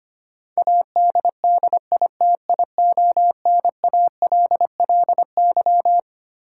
Morse code, heard as a keyed tone, about 25 wpm, 700 Hz. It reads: ADDITIONALLY
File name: Twelve Letter Words - 25wpm